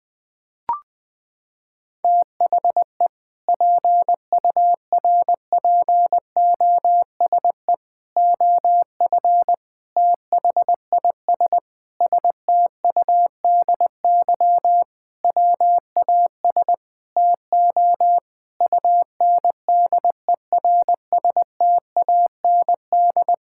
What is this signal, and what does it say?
Morse code, heard as a keyed tone, about 21 wpm, 700 Hz. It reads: THE PURPOSE OF THIS STUDY WAS TO UNDERSTAND
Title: Sentences with 5-Word Phrases - 20wpm